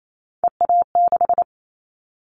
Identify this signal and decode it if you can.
Morse code, heard as a keyed tone, about 28 wpm, 700 Hz. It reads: EA6
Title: Callsign Prefixes for Top 50 DXCC - 28wpm